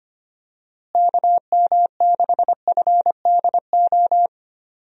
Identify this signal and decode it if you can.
Morse code, heard as a keyed tone, about 25 wpm, 700 Hz. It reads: KM6FDO
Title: USA Call Signs - 6 Characters - Repeated 2x - 25wpm